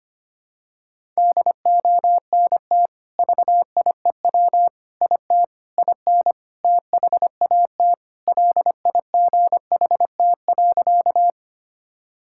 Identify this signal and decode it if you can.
Morse code, heard as a keyed tone, about 25 wpm, 700 Hz. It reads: DONT VIEW IT IN THAT LIGHT.
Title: Sentences from Top 1000 Words - Review Part 6 - 25wpm